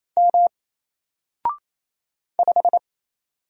Morse code, heard as a keyed tone, about 27 wpm, 700 Hz. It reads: M 5
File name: Single-Letter-Number-28wpm